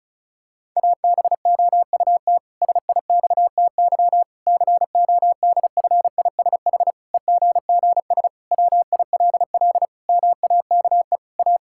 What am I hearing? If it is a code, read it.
Morse code, heard as a keyed tone, about 36 wpm, 700 Hz. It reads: ABOUT SIXTY CODFISH EGGS WILL MAKE A
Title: Pangram Challenge v2 - 35wpm